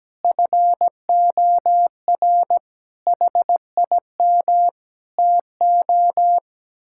Morse code, heard as a keyed tone, about 17 wpm, 700 Hz. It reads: FOR HIM TO
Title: Sentences from Top 1600 Words 17wpm